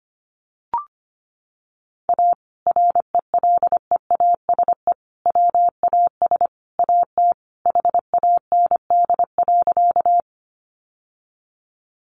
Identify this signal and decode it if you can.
Morse code, heard as a keyed tone, about 25 wpm, 690 Hz. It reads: A RELEASE WAS AT HAND.
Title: Sentences from Top 1100 Words 25wpm